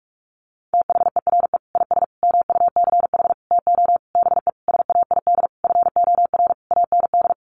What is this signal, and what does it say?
Morse code, heard as a keyed tone, about 44 wpm, 700 Hz. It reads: THERE IS MUCH TO BE SAID FOR AND
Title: Sentences from Top 1300 Words 45wpm